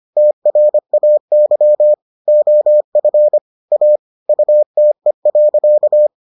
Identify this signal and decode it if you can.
Morse code, heard as a keyed tone, about 25 wpm, 600 Hz. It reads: TRAY OF A UTE.